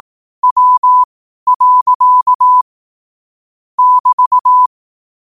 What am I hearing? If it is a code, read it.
Morse code, heard as a keyed tone, about 18 wpm, spaced as for 12 wpm, 1000 Hz. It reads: W. <BT>
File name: qotdfiles / 12wpm